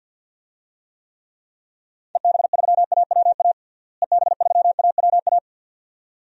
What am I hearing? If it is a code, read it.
Morse code, heard as a keyed tone, about 52 wpm, 700 Hz. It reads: EB3AWA EB3AWA
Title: DXCC-100 Call Signs - 6 Characters - Repeated 2x - 50wpm